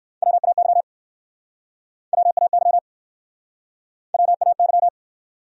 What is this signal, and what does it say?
Morse code, heard as a keyed tone, about 64 wpm, 700 Hz. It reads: WAY WAY WAY